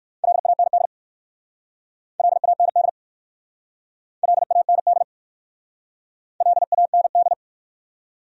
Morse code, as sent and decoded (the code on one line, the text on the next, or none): .-.. .- -. -.. / .-.. .- -. -.. / .-.. .- -. -.. / .-.. .- -. -..
LAND LAND LAND LAND